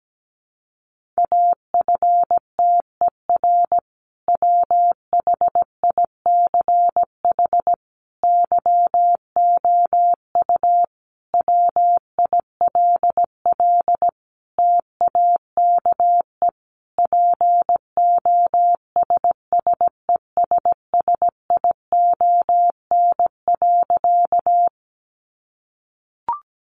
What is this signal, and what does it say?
Morse code, heard as a keyed tone, about 17 wpm, 700 Hz. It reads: AFTER WHICH YOU WILL TAKE POSSESSION.
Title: Sentences from Top 1400 Words 17wpm